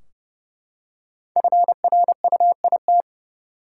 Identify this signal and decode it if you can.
Morse code, heard as a keyed tone, about 31 wpm, 700 Hz. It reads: FRUIT
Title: Five Letter Words - 30wpm